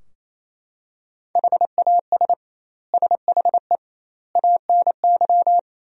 Morse code, heard as a keyed tone, about 28 wpm, 700 Hz. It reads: HAS SHE ANY
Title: 2x Word-Spacing - Sentences from Top 400 Words - 28wpm